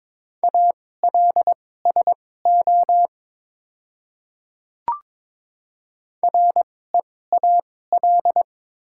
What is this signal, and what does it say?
Morse code, heard as a keyed tone, about 22 wpm, spaced as for 16 wpm, 700 Hz. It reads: ALSO REAL
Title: 2x Character Spacing - Top 300 Words - 22wpm